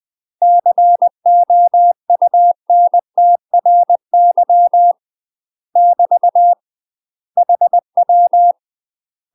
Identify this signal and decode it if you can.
Morse code, heard as a keyed tone, about 20 wpm, 700 Hz. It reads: COUNTRY <BT> HW
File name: UK MASTER QSOs TRAINING / 20WPM / UK MASTER QSOs 0074